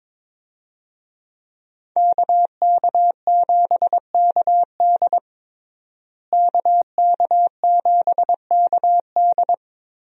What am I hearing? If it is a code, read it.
Morse code, heard as a keyed tone, about 21 wpm, 700 Hz. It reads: KK7KD KK7KD
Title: USA Call Signs - 5 Characters - Repeated 3x - 22wpm